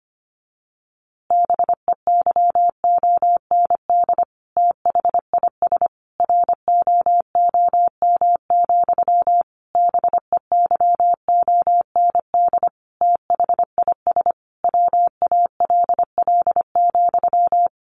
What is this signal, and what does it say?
Morse code, heard as a keyed tone, about 25 wpm, 700 Hz. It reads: BEYOND THIS ROOM, BEYOND THIS WALL,
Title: A Sound Of Thunder 25wpm